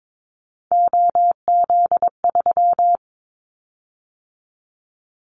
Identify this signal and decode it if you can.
Morse code, heard as a keyed tone, about 22 wpm, 700 Hz. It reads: OZ3
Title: Callsign Prefixes for Top 50 DXCC - 22wpm